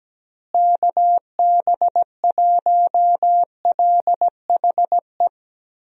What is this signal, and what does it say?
Morse code, heard as a keyed tone, about 17 wpm, 700 Hz. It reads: KB1LHE
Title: USA Call Signs - 6 Characters - Repeated 2x - 17wpm